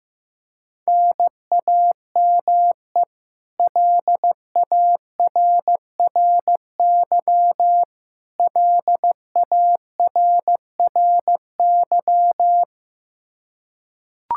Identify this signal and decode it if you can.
Morse code, heard as a keyed tone, about 14 wpm, 700 Hz. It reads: NAME LARRY LARRY
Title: Practice QSO - Exchange 1 - Beginning - 15wpm